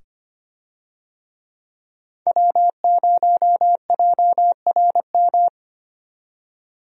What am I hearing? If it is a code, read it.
Morse code, heard as a keyed tone, about 25 wpm, 700 Hz. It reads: W0JRM
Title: USA Call Signs - 5 Characters - 25wpm